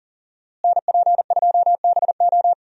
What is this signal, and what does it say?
Morse code, heard as a keyed tone, about 40 wpm, 700 Hz. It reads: NP2BO
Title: USA Call Signs - 5 Characters - Repeated 2x - 40wpm